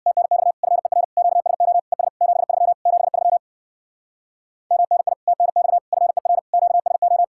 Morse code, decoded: TNX FER QSO ES 73 73 MNI TNX FER QSO

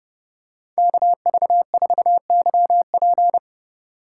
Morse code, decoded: KV4YP

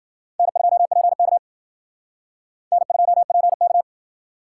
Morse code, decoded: N2PK N2PK

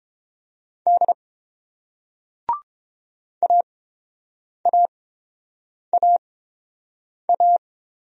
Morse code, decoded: D AAAA

